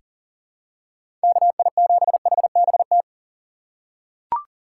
Morse code, heard as a keyed tone, about 40 wpm, 700 Hz. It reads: KI7HBT
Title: USA Call Signs - 6 Characters - 4x Character Spacing - 40wpm